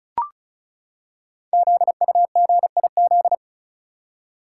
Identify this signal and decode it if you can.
Morse code, heard as a keyed tone, about 36 wpm, 700 Hz. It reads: ZUGIZ
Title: Code Groups - 5 Letter:Numbers - 35wpm